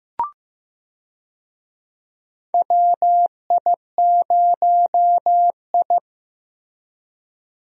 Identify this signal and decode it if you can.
Morse code, heard as a keyed tone, about 15 wpm, 700 Hz. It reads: WI0I
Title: USA Call Signs - 4 Characters - Repeated 2x - 15wpm